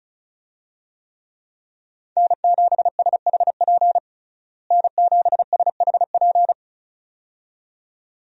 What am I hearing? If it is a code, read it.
Morse code, heard as a keyed tone, about 36 wpm, 700 Hz. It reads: N7SHP N7SHP
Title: USA Call Signs - 5 Characters - Repeated 2x - 35wpm